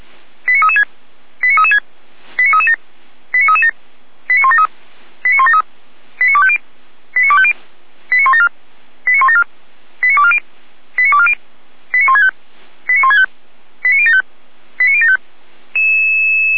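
Five-tone selective call (ZVEI), heard as quick runs of five tones, tones of about 80 ms. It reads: 89336 89336 89337 89337 89173 89173 89250 89250 89174 89174 89240 89240 89165 89165 89075 89075
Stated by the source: ZVEI five-tone selective call received by SDR, archived by Feuerwehr Orlamünde (fire brigade).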